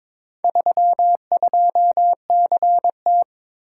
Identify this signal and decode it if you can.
Morse code, heard as a keyed tone, about 22 wpm, 700 Hz. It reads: 32CT
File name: practice / 22wpm / Four Letter-Number Combinations - 22wpm